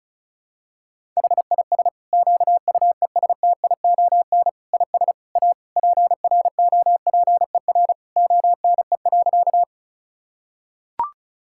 Morse code, heard as a keyed tone, about 36 wpm, 700 Hz. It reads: HIS QUESTION IS A PROPER ONE.
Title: Sentences from Top 1000 Words - Review Part 5 - 35wpm